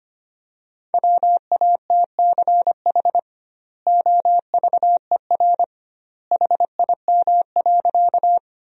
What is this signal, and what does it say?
Morse code, heard as a keyed tone, about 25 wpm, 700 Hz. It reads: WATCH OVER HIM.
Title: 2x Word-Spacing - Sentences from Top 300 Words - 25wpm